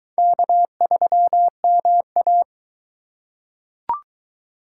Morse code, heard as a keyed tone, about 22 wpm, 700 Hz. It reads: K3MA